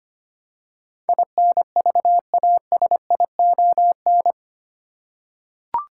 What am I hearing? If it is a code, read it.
Morse code, heard as a keyed tone, about 25 wpm, 700 Hz. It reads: INVASION
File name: Eight Letter Words - 25wpm